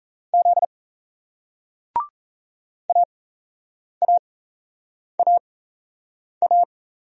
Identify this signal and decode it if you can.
Morse code, heard as a keyed tone, about 38 wpm, spaced as for 8 wpm, 700 Hz. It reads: Z AAAA